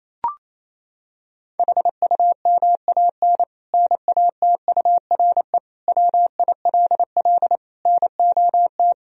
Morse code, heard as a keyed tone, about 28 wpm, 700 Hz. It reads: HUMAN NATURE WILL NOT
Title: Sentences from Top 1000 Words - Review Part 3 - 28wpm